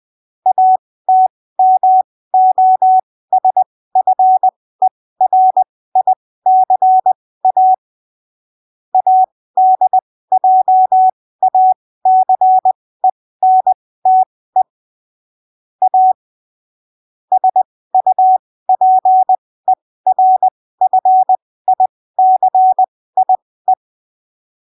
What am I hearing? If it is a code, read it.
Morse code, heard as a keyed tone, about 20 wpm, spaced as for 15 wpm, 750 Hz. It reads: ATMOSFERICA ADJACENTE A SUPERFICIE